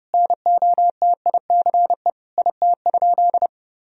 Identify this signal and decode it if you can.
Morse code, heard as a keyed tone, about 30 wpm, 700 Hz. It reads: NOTICE IT?
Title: Sentences from Top 1000 Words - Review Part 2 - 30wpm